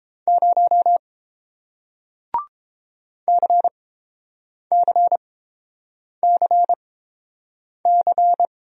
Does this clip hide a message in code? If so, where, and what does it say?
Yes: Morse code from 0.3 to 8.5 s, heard as a keyed tone, about 30 wpm, spaced as for 7 wpm, 700 Hz. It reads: 0 CCCC